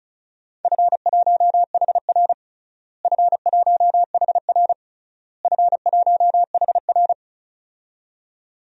Morse code, heard as a keyed tone, about 36 wpm, 700 Hz. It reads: F1HR F1HR F1HR
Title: DXCC-100 Call Signs - 4 Characters - Repeated 3x - 35wpm